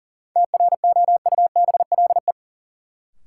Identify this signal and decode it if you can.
Morse code, heard as a keyed tone, about 40 wpm, 700 Hz. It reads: TROUBLE
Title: Sentences from Top 1200 Words - New Word Drill Alt - 40wpm